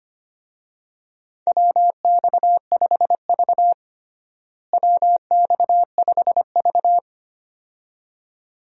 Morse code, heard as a keyed tone, about 25 wpm, 700 Hz. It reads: WX5V WX5V
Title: USA Call Signs - 4 Characters - Repeated 2x - 25wpm